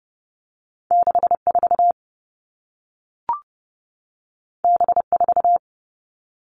Morse code, heard as a keyed tone, about 30 wpm, 700 Hz. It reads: 6 4 B 4
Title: Two Challenging Letter-Number Combinations 30wpm